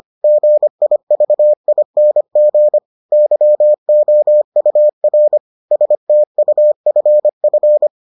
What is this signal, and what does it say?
Morse code, heard as a keyed tone, about 25 wpm, 600 Hz. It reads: GIVING YOUR STUFF